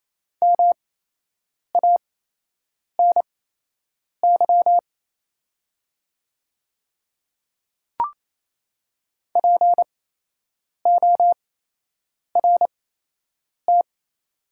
Morse code, heard as a keyed tone, about 28 wpm, spaced as for 8 wpm, 690 Hz. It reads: MANY PORT
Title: 8x Character Spacing - Top 200 Words - 28wpm